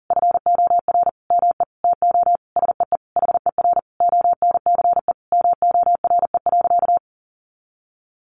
Morse code, heard as a keyed tone, about 37 wpm, 700 Hz. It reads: FOR ME TO SEE HER ONCE MORE.
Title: Sentences from Top 500 Words 40wpm